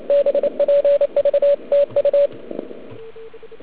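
Morse code, heard as a keyed tone, about 28 wpm, 570 Hz. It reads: BPVTU
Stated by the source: Morse from the OK QRP 2013 contest, run at 1 watt on a 12.3V lithium battery, Liberci, with OK1MKX also at 1W.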